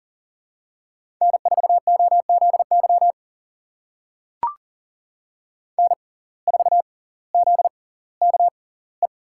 Morse code, heard as a keyed tone, about 38 wpm, spaced as for 20 wpm, 700 Hz. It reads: N4OZY N4ZKE
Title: USA Call Signs - 5 Characters - 6x Character Spacing - 40wpm